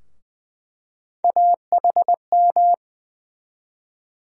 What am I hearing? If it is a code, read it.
Morse code, heard as a keyed tone, about 20 wpm, 700 Hz. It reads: AHM
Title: USA Call Sign Postfixes - 3 Letters 20wpm